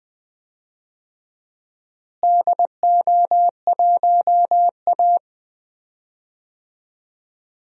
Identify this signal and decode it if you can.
Morse code, heard as a keyed tone, about 20 wpm, 700 Hz. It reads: DO1A
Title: DXCC-100 Call Signs - 4 Characters - Repeated 3x - 20wpm